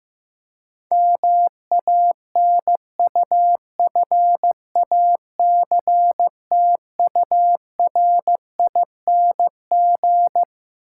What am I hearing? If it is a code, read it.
Morse code, heard as a keyed tone, about 15 wpm, 700 Hz. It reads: MANUFACTURING
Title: K1USN SST QSO Practice 15wpm